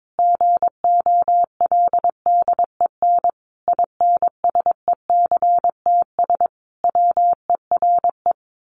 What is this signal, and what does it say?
Morse code, heard as a keyed tone, about 22 wpm, 700 Hz. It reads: GOLDEN INSECTS WERE
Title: A Sound Of Thunder 22wpm